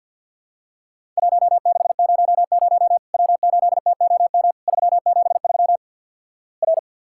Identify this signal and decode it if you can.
Morse code, heard as a keyed tone, about 50 wpm, 700 Hz. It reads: 1600 W8TOM 373 R
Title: Worked All Europe DX Contest - QTC Exchange with Repeats - 50wpm